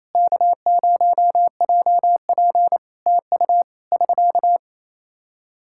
Morse code, heard as a keyed tone, about 28 wpm, 700 Hz. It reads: K0JP TU <SK>
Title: Practice QSO - Exchange 4 - Ending - 28wpm